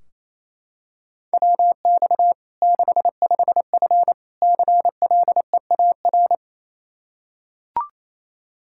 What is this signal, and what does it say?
Morse code, heard as a keyed tone, about 28 wpm, 700 Hz. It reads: WX 65F CLEAR
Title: Practice QSO - Exchange 2 - Info - 28wpm